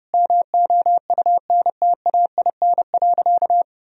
Morse code, heard as a keyed tone, about 30 wpm, 700 Hz. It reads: MOUNTAIN.
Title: Sentences from Top 1000 Words - Review Part 2 - 30wpm